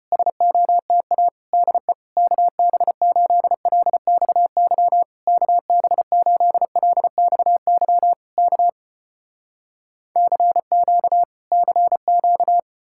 Morse code, heard as a keyed tone, about 36 wpm, 700 Hz. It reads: SOTA DE KB8LXY KB8LXY K CQ CQ